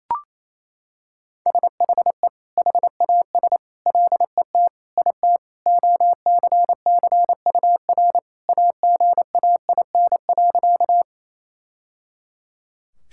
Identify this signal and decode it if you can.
Morse code, heard as a keyed tone, about 28 wpm, 700 Hz. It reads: SHE HAS LET IT OCCUR AGAIN.
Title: Sentences from Top 1000 Words - Review Part 6 - 28wpm